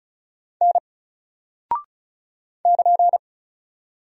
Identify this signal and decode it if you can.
Morse code, heard as a keyed tone, about 35 wpm, 700 Hz. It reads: N <KN>
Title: Single Character (Letter:Number:Punctuation:Prosign) 35wpm